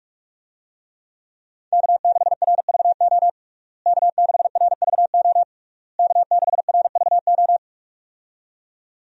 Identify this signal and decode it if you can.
Morse code, heard as a keyed tone, about 47 wpm, 700 Hz. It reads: K6RVO K6RVO K6RVO
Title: USA Call Signs - 5 Characters - Repeated 3x - 45wpm